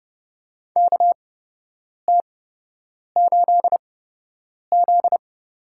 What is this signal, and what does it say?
Morse code, heard as a keyed tone, about 30 wpm, spaced as for 8 wpm, 700 Hz. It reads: KT8Z
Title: USA Call Signs - 4 Characters - 8x Character Spacing - 30wpm